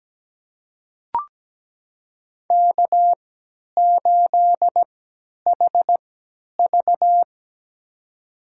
Morse code, heard as a keyed tone, about 17 wpm, spaced as for 10 wpm, 700 Hz. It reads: K8HV
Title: USA Call Signs - 4 Characters - 3x Character Spacing - 17wpm